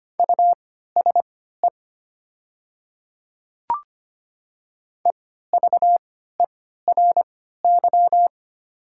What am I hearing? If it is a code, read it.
Morse code, heard as a keyed tone, about 25 wpm, spaced as for 14 wpm, 690 Hz. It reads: USE EVERY
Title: 3x Character Spacing - Top 300 Words - 25wpm